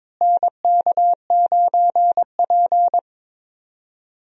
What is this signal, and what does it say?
Morse code, heard as a keyed tone, about 22 wpm, 700 Hz. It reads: NK9P
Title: USA Call Signs - 4 Characters - Repeated 2x - 22wpm